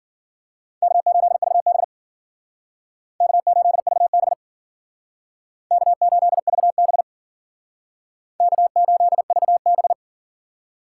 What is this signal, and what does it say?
Morse code, heard as a keyed tone, about 51 wpm, 700 Hz. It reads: K8VB K8VB K8VB K8VB